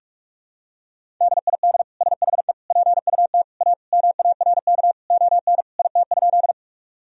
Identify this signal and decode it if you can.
Morse code, heard as a keyed tone, about 46 wpm, 700 Hz. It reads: DID SHE PUT A MARK ON IT?
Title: Sentences from Top 1000 Words - Review Part 1 - 45wpm